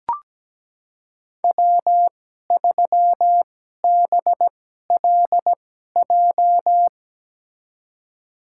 Morse code, heard as a keyed tone, about 17 wpm, spaced as for 12 wpm, 700 Hz. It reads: W3BLJ
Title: USA Call Signs - 5 Characters - 2x Character Spacing - 17wpm